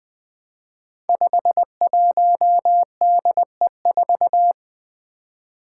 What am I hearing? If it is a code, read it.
Morse code, heard as a keyed tone, about 20 wpm, 700 Hz. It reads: 51DE4
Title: Five Challenging Letter-Number Combinations 20wpm